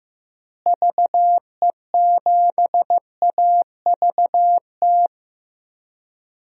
Morse code, heard as a keyed tone, about 15 wpm, 700 Hz. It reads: VE7AVT
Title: DXCC-100 Call Signs - 6 Characters - Repeated 3x - 15wpm